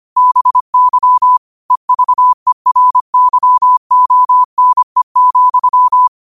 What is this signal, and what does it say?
Morse code, heard as a keyed tone, about 25 wpm, 1000 Hz. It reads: BY EVERYONE,
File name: qotdfiles / 25wpm